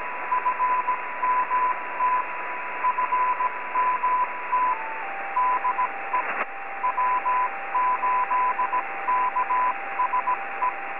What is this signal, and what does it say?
Morse code, heard as a keyed tone, about 17 wpm, 1000 Hz, with radio noise. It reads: FMT FMT DE W8KSE
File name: 80m-snip